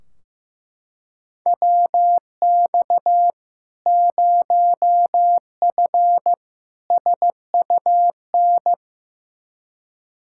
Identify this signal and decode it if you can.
Morse code, heard as a keyed tone, about 15 wpm, 700 Hz. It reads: WX 0F SUN
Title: Practice QSO - Exchange 2 - Info - 15wpm